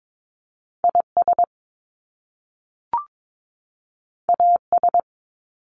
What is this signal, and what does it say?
Morse code, heard as a keyed tone, about 22 wpm, 700 Hz. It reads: IS AS
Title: Top Two Letter Combinations in Words 22wpm